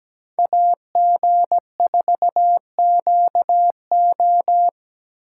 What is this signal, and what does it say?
Morse code, heard as a keyed tone, about 17 wpm, 700 Hz. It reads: AG4QO
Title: USA Call Signs - 5 Characters - 8x Character Spacing - 17wpm